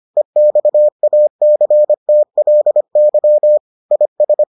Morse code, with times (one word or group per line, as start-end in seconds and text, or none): 0.2-3.6 EXACTLY
3.9-4.4 IS